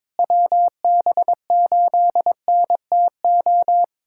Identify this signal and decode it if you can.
Morse code, heard as a keyed tone, about 22 wpm, 700 Hz. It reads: WB8NTO
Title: USA Callsign DE Callsign 22wpm